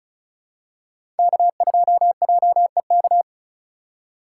Morse code, heard as a keyed tone, about 36 wpm, 700 Hz. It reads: K2JEK
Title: USA Call Signs - 4x Character Spacing - 35wpm